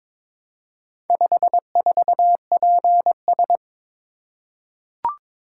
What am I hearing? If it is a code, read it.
Morse code, heard as a keyed tone, about 22 wpm, 700 Hz. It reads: 54PS
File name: practice / 22wpm / Four Letter-Number Combinations - 22wpm